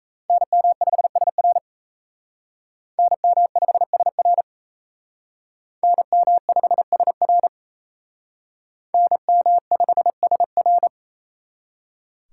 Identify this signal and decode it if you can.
Morse code, heard as a keyed tone, about 35 wpm, 700 Hz. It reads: NM5SR NM5SR NM5SR NM5SR
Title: Speed Racing - USA Call Signs 28wpm